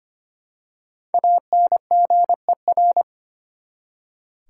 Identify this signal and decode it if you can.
Morse code, heard as a keyed tone, about 25 wpm, 700 Hz. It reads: ANGER
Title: Five Letter Words - 25wpm